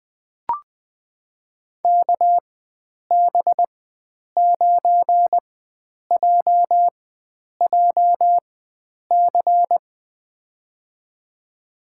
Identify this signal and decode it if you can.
Morse code, heard as a keyed tone, about 20 wpm, spaced as for 9 wpm, 700 Hz. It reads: KB9JJC
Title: USA Call Signs - 4x Character Spacing - 20wpm